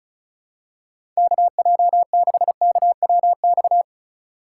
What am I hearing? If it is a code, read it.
Morse code, heard as a keyed tone, about 36 wpm, 700 Hz. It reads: KJ6KWX
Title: USA Call Signs - 6 Characters - 3x Character Spacing - 35wpm